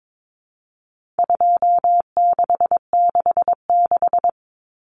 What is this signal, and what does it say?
Morse code, heard as a keyed tone, about 22 wpm, 700 Hz. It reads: 2666
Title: Common Cut Numbers - 4-Digit 22wpm